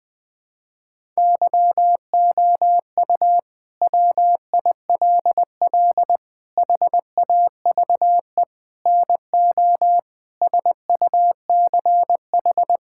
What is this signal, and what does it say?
Morse code, heard as a keyed tone, about 21 wpm, 700 Hz. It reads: YOU WILL HAVE NO SUCH
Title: Sentences from Top 1000 Words - Review Part 6 - 20wpm